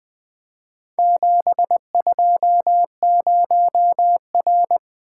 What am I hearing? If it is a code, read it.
Morse code, heard as a keyed tone, about 20 wpm, 700 Hz. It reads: 720R
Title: Four Letter-Number Combinations - 20wpm